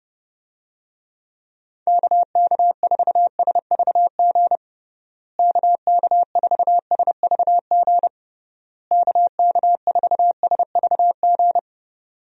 Morse code, heard as a keyed tone, about 30 wpm, 700 Hz. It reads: KK4SVG KK4SVG KK4SVG
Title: USA Call Signs - 6 Characters - Repeated 3x - 30wpm